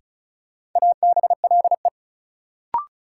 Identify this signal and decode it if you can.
Morse code, heard as a keyed tone, about 36 wpm, 700 Hz. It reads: ABLE
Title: Sentences from Top 700 Words - New Word Drill - 35wpm